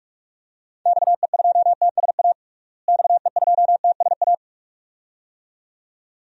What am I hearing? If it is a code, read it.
Morse code, heard as a keyed tone, about 46 wpm, 700 Hz. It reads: XE2TSA XE2TSA
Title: DXCC-100 Call Signs - 6 Characters - Repeated 2x - 45wpm